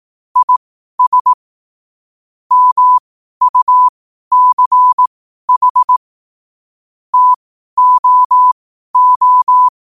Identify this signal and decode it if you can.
Morse code, heard as a keyed tone, about 18 wpm, spaced as for 12 wpm, 1000 Hz. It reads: IS MUCH TOO
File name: 12wpm